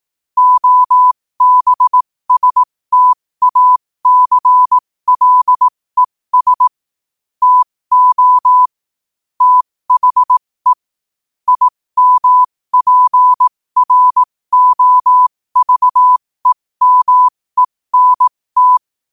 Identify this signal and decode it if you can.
Morse code, heard as a keyed tone, about 18 wpm, spaced as for 15 wpm, 1000 Hz. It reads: OBSTACLES TO THE IMPROVEMENT
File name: qotdfiles / 15wpm